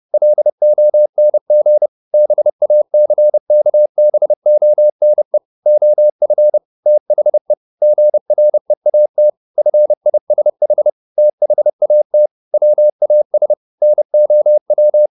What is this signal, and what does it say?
Morse code, heard as a keyed tone, about 30 wpm, 600 Hz. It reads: LONG BACKBONE OF THE GREAT FISH THAT WAS NOW